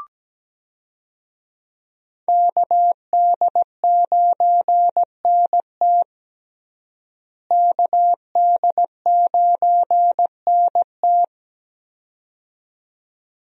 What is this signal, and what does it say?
Morse code, heard as a keyed tone, about 17 wpm, 700 Hz. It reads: KD9NT KD9NT
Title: USA Call Signs - 5 Characters - Repeated 2x - 17wpm